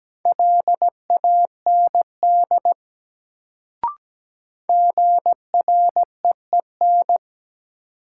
Morse code, heard as a keyed tone, about 17 wpm, 700 Hz. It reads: LAND GREEN
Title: Sentences from Top 500 Words - New Word Drill - 17wpm